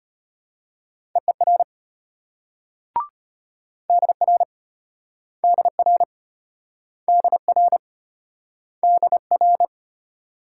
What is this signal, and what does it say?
Morse code, heard as a keyed tone, about 33 wpm, 700 Hz. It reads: EER DR DR DR DR